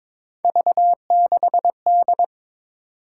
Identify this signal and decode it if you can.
Morse code, heard as a keyed tone, about 22 wpm, 700 Hz. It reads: V6D
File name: practice / 22wpm / Three Letter-Number Combinations - 22wpm